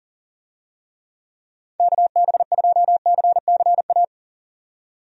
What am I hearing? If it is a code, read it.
Morse code, heard as a keyed tone, about 40 wpm, 700 Hz. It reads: KB2CCA
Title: USA Call Signs - 6 Characters - 40wpm